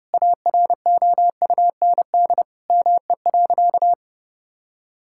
Morse code, heard as a keyed tone, about 30 wpm, 700 Hz. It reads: AROUND ME.
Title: Sentences from Top 1600 Words 30wpm